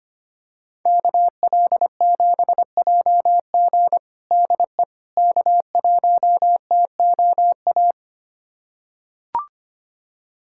Morse code, heard as a keyed tone, about 25 wpm, 700 Hz. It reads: KL7JG DE K1TOA
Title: USA Callsign DE Callsign - 5 Characters - 25wpm